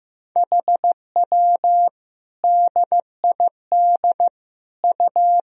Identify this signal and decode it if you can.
Morse code, heard as a keyed tone, about 15 wpm, 700 Hz. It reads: HW DID U
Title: Sentences from Top 100 Words and Abbreviations 15wpm